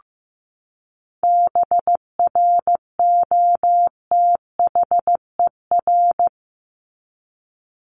Morse code, heard as a keyed tone, about 15 wpm, 700 Hz. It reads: BROTHER
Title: Top 1751-2000 Words 15wpm